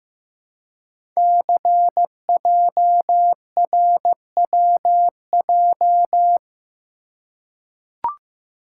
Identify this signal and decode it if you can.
Morse code, heard as a keyed tone, about 15 wpm, 700 Hz. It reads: CJRWJ
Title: Code Groups - 5 Letters - 15wpm